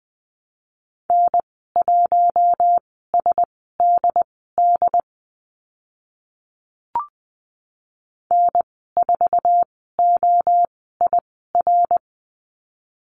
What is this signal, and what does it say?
Morse code, heard as a keyed tone, about 20 wpm, spaced as for 14 wpm, 700 Hz. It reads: N1SDD N4OIR
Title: 069 Lesson 12g - Letter d - Call Signs 20wpm